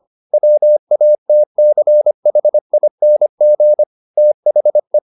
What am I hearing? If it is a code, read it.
Morse code, heard as a keyed tone, about 25 wpm, 600 Hz. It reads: WATCHING THE